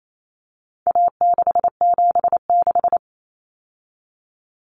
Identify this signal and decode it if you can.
Morse code, heard as a keyed tone, about 28 wpm, 700 Hz. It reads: A676
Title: Common Cut Numbers - 4-Digit 28wpm